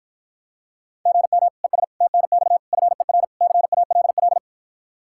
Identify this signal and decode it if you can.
Morse code, heard as a keyed tone, about 54 wpm, 700 Hz. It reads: GM ES TNX FER CALL